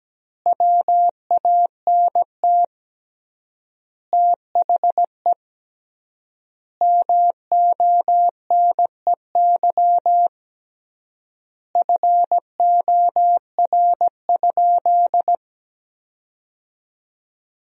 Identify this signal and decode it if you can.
Morse code, heard as a keyed tone, about 17 wpm, 700 Hz. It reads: WANT THE MONEY FOR?
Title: 3x Word-Spacing - Sentences from Top 400 Words - 17wpm